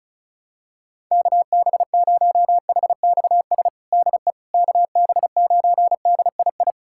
Morse code, heard as a keyed tone, about 36 wpm, 700 Hz. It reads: KB0HXS DE KB9DII
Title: USA Callsign DE Callsign - 6 Characters - 35wpm